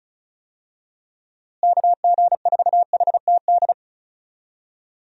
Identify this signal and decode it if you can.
Morse code, heard as a keyed tone, about 34 wpm, 700 Hz. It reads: KG4HTD
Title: USA Call Signs - 6 Characters - Repeated 2x - 35wpm